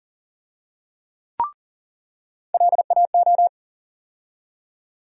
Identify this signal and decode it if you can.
Morse code, heard as a keyed tone, about 39 wpm, 700 Hz. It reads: LAO
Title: USA Call Sign Postfixes - 3 Letters 40wpm